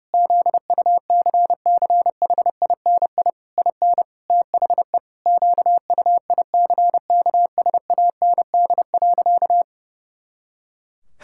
Morse code, heard as a keyed tone, about 30 wpm, 700 Hz. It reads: ZUCCHINI IN THE QUICKSAND.
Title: Pangram Challenge v2 - 30wpm